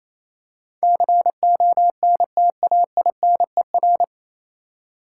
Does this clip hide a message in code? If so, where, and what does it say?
Yes: Morse code from 0.8 to 4.0 s, heard as a keyed tone, about 28 wpm, 690 Hz. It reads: CONTAINER